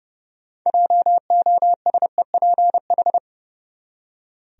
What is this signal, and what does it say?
Morse code, heard as a keyed tone, about 30 wpm, 700 Hz. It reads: JOSEPH
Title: USA Common Names - 30wpm